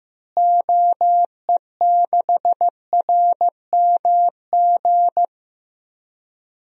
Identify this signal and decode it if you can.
Morse code, heard as a keyed tone, about 16 wpm, 700 Hz. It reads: OE6RMG
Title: DXCC-100 Call Signs - 6 Characters - Repeated 2x - 15wpm